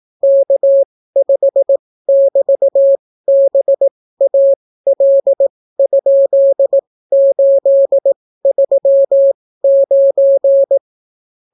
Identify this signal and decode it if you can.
Morse code, heard as a keyed tone, about 18 wpm, spaced as for 15 wpm, 550 Hz. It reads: K5<BT>BAL?839